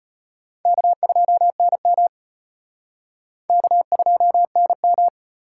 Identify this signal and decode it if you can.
Morse code, heard as a keyed tone, about 31 wpm, 700 Hz. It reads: K2NM K2NM
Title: Speed Racing - USA 1x2 Call Signs 25wpm